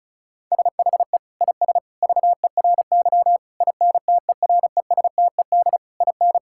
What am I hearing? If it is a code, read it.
Morse code, heard as a keyed tone, about 34 wpm, 700 Hz. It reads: SHE IS VERY INTERESTED IN
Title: Sentences from Top 1000 Words - Review Part 6 - 35wpm